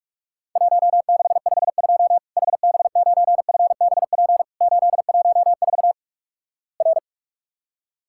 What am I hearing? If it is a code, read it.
Morse code, heard as a keyed tone, about 46 wpm, 700 Hz. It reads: 1652 HB9FBP 814 R
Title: Worked All Europe DX Contest - QTC Exchange - 45wpm